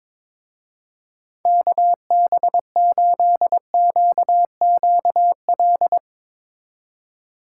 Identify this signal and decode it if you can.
Morse code, heard as a keyed tone, about 22 wpm, 700 Hz. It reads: KB8QQL
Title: Speed Racing - USA 2x3 Call Signs 22wpm